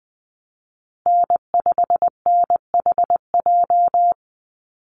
Morse code, heard as a keyed tone, about 20 wpm, 700 Hz. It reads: N5NHJ
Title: Call Signs Repeated 2x - 20wpm